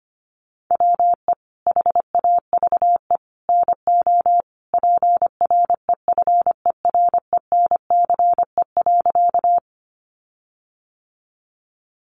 Morse code, heard as a keyed tone, about 25 wpm, 700 Hz. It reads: WE HAVE NO PREFERENCE.
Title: Sentences from Top 1500 Words 25wpm